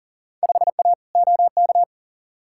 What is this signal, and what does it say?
Morse code, heard as a keyed tone, about 40 wpm, 700 Hz. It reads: 5A OK
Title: Field Day Exchanges - 40wpm